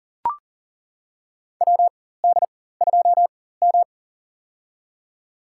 Morse code, heard as a keyed tone, about 40 wpm, spaced as for 19 wpm, 700 Hz. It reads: WD2M
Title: USA Call Signs - 4 Characters - 4x Character Spacing - 40wpm